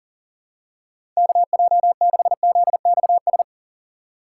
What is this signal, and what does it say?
Morse code, heard as a keyed tone, about 40 wpm, 700 Hz. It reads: KJ6ZXS
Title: USA Call Signs - 6 Characters - 4x Character Spacing - 40wpm